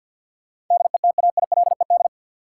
Morse code, heard as a keyed tone, about 49 wpm, 700 Hz. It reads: DETAILED
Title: Sentences from Top 1500 Words - New Word Drill Alt - 50wpm